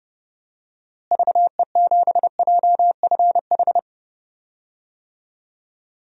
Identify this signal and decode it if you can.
Morse code, heard as a keyed tone, about 31 wpm, 700 Hz. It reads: VE7JFH
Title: DXCC-100 Call Signs - 6 Characters - Repeated 2x - 30wpm